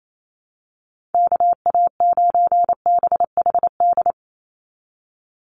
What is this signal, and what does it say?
Morse code, heard as a keyed tone, about 28 wpm, 700 Hz. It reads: KA9BHD
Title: Call Signs Repeated 2x - 28wpm